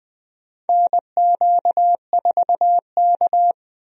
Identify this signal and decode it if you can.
Morse code, heard as a keyed tone, about 20 wpm, 700 Hz. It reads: NQ4K
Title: USA Call Signs - 4 Characters - 20wpm